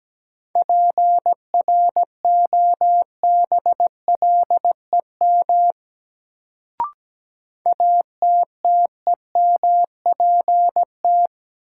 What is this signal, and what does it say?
Morse code, heard as a keyed tone, about 17 wpm, 700 Hz. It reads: PROBLEM ATTEMPT
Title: Sentences from Top 1000 Words - New Word Drill - 17wpm